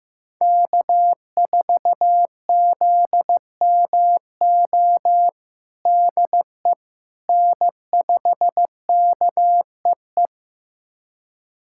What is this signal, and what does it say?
Morse code, heard as a keyed tone, about 15 wpm, 700 Hz. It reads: K4ZMO DE N5KEE
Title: USA Callsign DE Callsign - 5 Characters - 15wpm